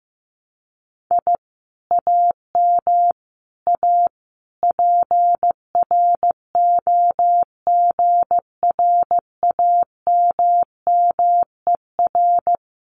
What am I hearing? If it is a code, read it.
Morse code, heard as a keyed tone, about 15 wpm, 700 Hz. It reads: I AM A PROGRAMMER
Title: Common QSO Phrases Encore 15wpm